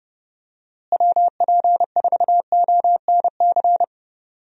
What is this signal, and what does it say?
Morse code, heard as a keyed tone, about 29 wpm, 700 Hz. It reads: WP4ONC
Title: USA Call Signs - 6 Characters - Repeated 3x - 30wpm